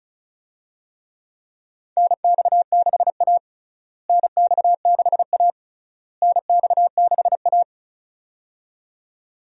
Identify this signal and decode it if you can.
Morse code, heard as a keyed tone, about 34 wpm, 700 Hz. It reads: NX6A NX6A NX6A
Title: USA Call Signs - 4 Characters - Repeated 3x - 35wpm